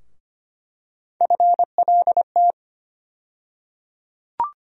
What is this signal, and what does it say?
Morse code, heard as a keyed tone, about 25 wpm, 700 Hz. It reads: FLT
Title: USA Call Sign Postfixes - 3 Letters 25wpm